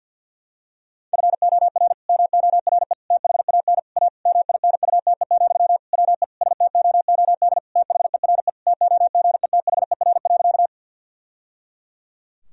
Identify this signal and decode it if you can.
Morse code, heard as a keyed tone, about 47 wpm, 700 Hz. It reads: FOR MORE THAN A MINUTE, WE STOOD THERE TOGETHER.
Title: Sentences from Top 1000 Words - Review Part 2 - 50wpm